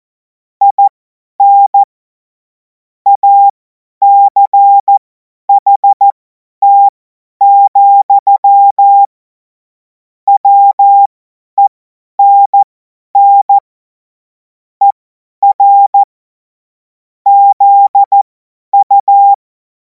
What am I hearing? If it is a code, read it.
Morse code, heard as a keyed tone, about 14 wpm, spaced as for 10 wpm, 800 Hz. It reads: IN ACHT, WENN ER ZU
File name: Morsekurs / Witze / Witz 03. März 2026 50 F70